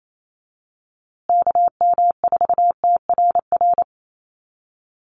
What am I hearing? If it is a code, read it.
Morse code, heard as a keyed tone, about 28 wpm, 700 Hz. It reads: KM4TRR
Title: USA Call Signs Repeated 2x 28wpm